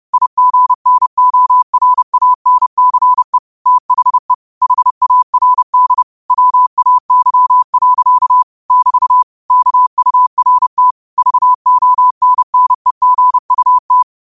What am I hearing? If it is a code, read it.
Morse code, heard as a keyed tone, about 30 wpm, 1000 Hz. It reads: IGNORANCE THE HARD WAY. <BT> KURT VONNEGUT